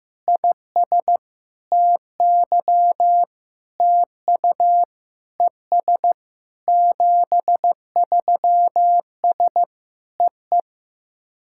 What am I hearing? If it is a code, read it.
Morse code, heard as a keyed tone, about 15 wpm, 700 Hz. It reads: IS TY TU ES 73S EE
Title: SOTA QSO Practice 15wpm